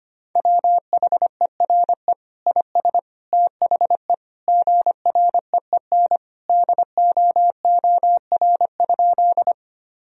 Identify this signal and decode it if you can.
Morse code, heard as a keyed tone, about 25 wpm, 700 Hz. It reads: WHERE IS THE GREEN DOOR?
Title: Sentences from Top 1000 Words - Review Part 2 - 25wpm